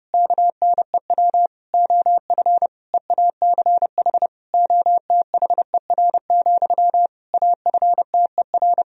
Morse code, heard as a keyed tone, about 30 wpm, 700 Hz. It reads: KNEW OF EACH OTHER, AFTER
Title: Sentences from Top 1000 Words - Review Part 1 - 30wpm